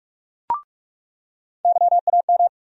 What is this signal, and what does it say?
Morse code, heard as a keyed tone, about 48 wpm, 700 Hz. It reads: YAM
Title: USA Call Sign Postfixes - 3 Letters 45wpm